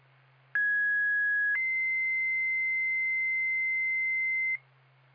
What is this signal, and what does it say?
Two-tone page, heard as a one-second tone, then a three-second tone, about 1650 then 2050 Hz, with noise. It is heard from 0.5 to 4.6 s.
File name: Stillwater FD 2026 07 26 14 43 52